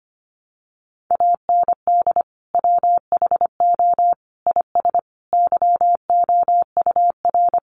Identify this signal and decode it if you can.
Morse code, heard as a keyed tone, about 25 wpm, 700 Hz. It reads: AND WHO IS YOUR
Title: Sentences from Top 1300 Words 25wpm